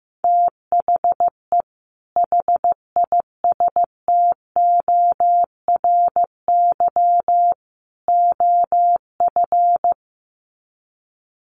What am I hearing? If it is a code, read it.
Morse code, heard as a keyed tone, about 15 wpm, 700 Hz. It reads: THE HISTORY OF
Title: Sets of 3 Words 15wpm